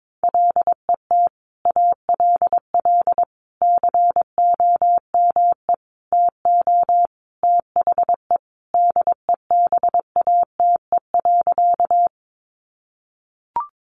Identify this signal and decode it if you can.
Morse code, heard as a keyed tone, about 22 wpm, 700 Hz. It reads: LET ALL COME TO THE DEBATE.
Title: Sentences from Top 1100 Words 22wpm